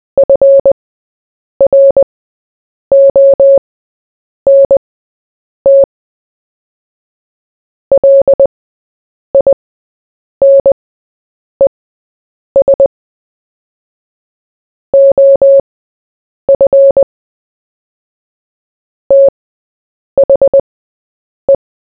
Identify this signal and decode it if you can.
Morse code, heard as a keyed tone, about 20 wpm, spaced as for 8 wpm, 550 Hz. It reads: FRONT LINES OF THE
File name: Short-Story-4-8wpm